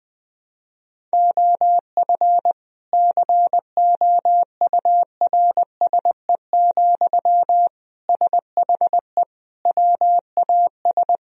Morse code, heard as a keyed tone, about 20 wpm, 700 Hz. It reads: OF COURSE, SHE WAS
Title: Sentences from Top 1000 Words - Review Part 4 - 20wpm